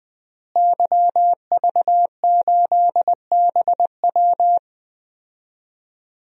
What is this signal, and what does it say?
Morse code, heard as a keyed tone, about 20 wpm, 700 Hz. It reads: YV8BW
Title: DXCC-100 Call Signs - 5 Characters - Repeated 2x - 20wpm